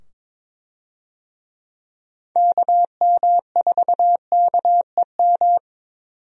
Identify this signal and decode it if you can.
Morse code, heard as a keyed tone, about 21 wpm, 700 Hz. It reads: KM4KEM
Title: USA Call Signs - 6 Characters - 22wpm